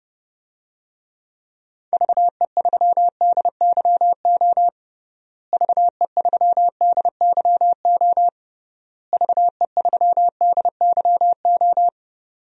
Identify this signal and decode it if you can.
Morse code, heard as a keyed tone, about 32 wpm, 700 Hz. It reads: VE3DYO VE3DYO VE3DYO
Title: DXCC-100 Call Signs - 6 Characters - Repeated 3x - 30wpm